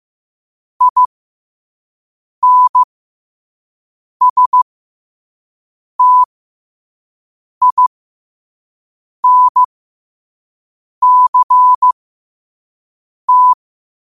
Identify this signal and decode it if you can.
Morse code, heard as a keyed tone, about 15 wpm, spaced as for 5 wpm, 1000 Hz. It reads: INSTINCT
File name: qotdfiles / 5wpm